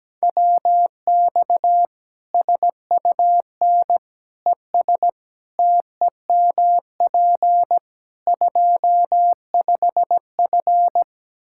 Morse code, heard as a keyed tone, about 17 wpm, 700 Hz. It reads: WX SUN ES TEMP 25F
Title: Practice QSO - Exchange 2 - Info - 17wpm